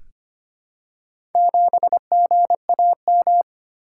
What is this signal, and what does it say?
Morse code, heard as a keyed tone, about 25 wpm, 700 Hz. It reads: 7GAM
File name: practice / 25wpm / Four Letter-Number Combinations - 25wpm